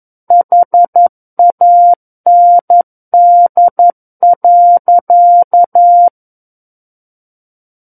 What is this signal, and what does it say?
Morse code, heard as a keyed tone, about 11 wpm, 700 Hz. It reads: HAND.
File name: Book One / 02-Chapter 2 - 11 WPM